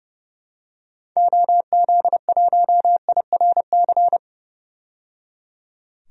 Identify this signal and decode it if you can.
Morse code, heard as a keyed tone, about 30 wpm, 700 Hz. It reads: OZ1IRC
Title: DXCC-100 Call Signs - 6 Characters - 30wpm